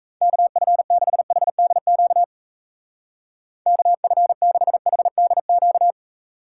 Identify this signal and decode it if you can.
Morse code, heard as a keyed tone, about 36 wpm, 700 Hz. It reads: KF6HDQ KF6HDQ